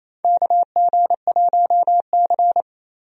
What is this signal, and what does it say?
Morse code, heard as a keyed tone, about 28 wpm, 700 Hz. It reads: KG1C
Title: USA Call Signs - 4 Characters - Repeated 3x - 28wpm